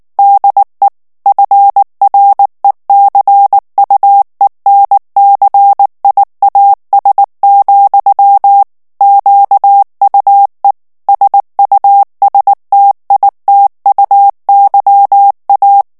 Morse code, heard as a keyed tone, about 19 wpm, 800 Hz. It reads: DE FRECUENCIAS, QUE SUSTITUYA
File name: morse mp3 / ESPAGNOL TEXTE EN CLAIR 2 VITESSE 18 MOTS MINUTES